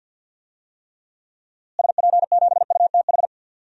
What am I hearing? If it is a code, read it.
Morse code, heard as a keyed tone, about 49 wpm, 700 Hz. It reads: SP7UTH
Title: DXCC-100 Call Signs - 6 Characters - Repeated 2x - 50wpm